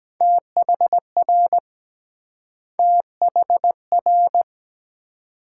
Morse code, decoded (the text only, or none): THR THR